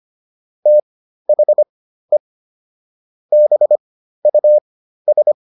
Morse code, heard as a keyed tone, about 25 wpm, spaced as for 13 wpm, 600 Hz. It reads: THE BUS